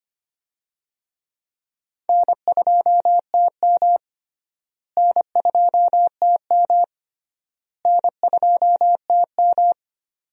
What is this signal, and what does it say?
Morse code, heard as a keyed tone, about 25 wpm, 700 Hz. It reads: N2TM N2TM N2TM
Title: USA Call Signs - 4 Characters - Repeated 3x - 25wpm